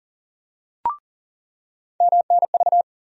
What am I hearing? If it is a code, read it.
Morse code, heard as a keyed tone, about 40 wpm, 700 Hz. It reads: MNV
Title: USA Call Sign Postfixes - 3 Letters 40wpm